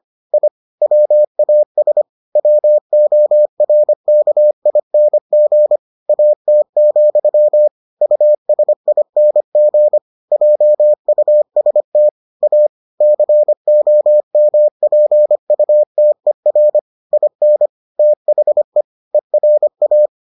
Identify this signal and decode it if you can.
Morse code, heard as a keyed tone, about 25 wpm, 600 Hz. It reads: I WAS WORKING AT, USING JUST A COMPUTER IN THE ERA